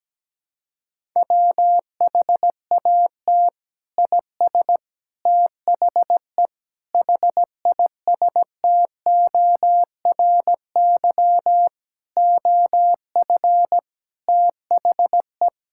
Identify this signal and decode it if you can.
Morse code, heard as a keyed tone, about 17 wpm, 700 Hz. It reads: WHAT IS THE HISTORY OF THE
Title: Sentences from Top 1000 Words - Review Part 5 - 17wpm